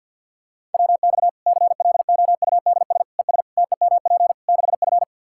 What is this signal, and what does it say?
Morse code, heard as a keyed tone, about 48 wpm, 700 Hz. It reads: WX CLOUDS ES TEMP 6F
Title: Practice QSO - Exchange 2 - Info - 50wpm